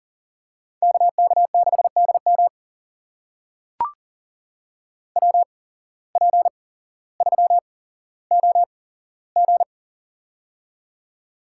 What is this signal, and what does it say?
Morse code, heard as a keyed tone, about 40 wpm, spaced as for 17 wpm, 700 Hz. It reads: KK6DM WP3OG
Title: USA Call Signs - 5 Characters - 8x Character Spacing - 40wpm